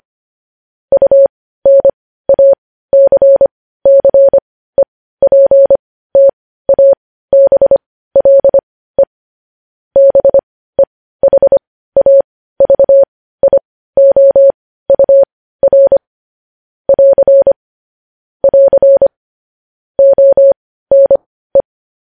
Morse code, heard as a keyed tone, about 25 wpm, spaced as for 15 wpm, 550 Hz. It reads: UNACCEPTABLE BEHAVIOUR <AR> <AR> ONE
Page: news-headlines-in-morse-code-at-15-wpm